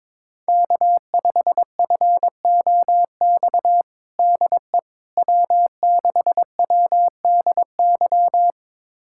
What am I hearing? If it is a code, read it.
Morse code, heard as a keyed tone, about 22 wpm, 700 Hz. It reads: K5FOX DE W6WDY
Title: USA Callsign DE Callsign - 5 Characters - 22wpm